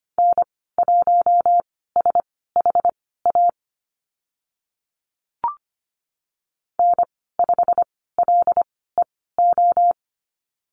Morse code, heard as a keyed tone, about 25 wpm, spaced as for 16 wpm, 700 Hz. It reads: N1SHA N5LEO
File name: Lesson 15g - Number 5 - Call Signs 25wpm